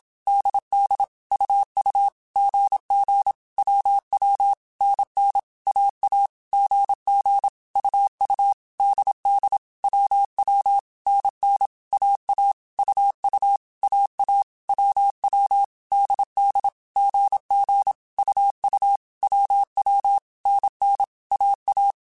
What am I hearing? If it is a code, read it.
Morse code, heard as a keyed tone, about 27 wpm, 800 Hz. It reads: DD UU GG WW NN AA GG UU DD WW NN AA UU AA WW DD GG UU WW NN AA
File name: Basic Sets / Basic Letter Groups Collective7 / AWUNDG@25